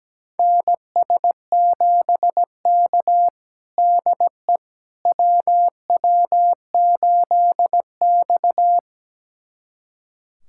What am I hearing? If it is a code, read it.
Morse code, heard as a keyed tone, about 17 wpm, 700 Hz. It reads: NS7K DE WW8X
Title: USA Callsign DE Callsign - 4 Characters - 17wpm